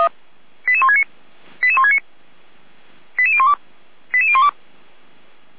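Five-tone selective call (ZVEI), heard as quick runs of five tones, tones of about 70 ms. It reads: 88169 88169 88012 88012
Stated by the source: ZVEI five-tone selective call received by SDR, from the Feuerwehr Orlamünde fire brigade.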